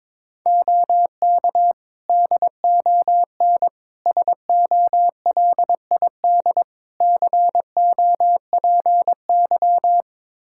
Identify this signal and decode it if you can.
Morse code, heard as a keyed tone, about 22 wpm, 700 Hz. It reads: OK DON SOLID COPY
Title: Practice QSO - Exchange 3 - More Info - 22wpm